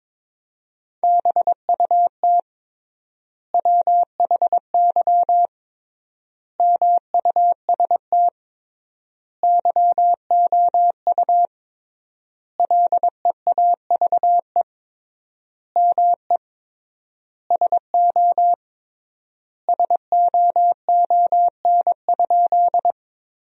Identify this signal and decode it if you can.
Morse code, heard as a keyed tone, about 22 wpm, 700 Hz. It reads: BUT WHY MUST YOU LEAVE ME SO SOON?
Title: 3x Word-Spacing - Sentences from Top 400 Words - 22wpm